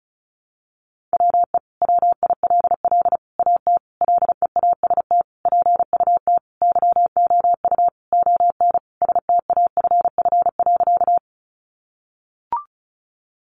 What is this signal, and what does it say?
Morse code, heard as a keyed tone, about 34 wpm, 700 Hz. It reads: WE WILL AT LEAST PUT YOU ON STAFF.
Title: Sentences from Top 1200 Words 35wpm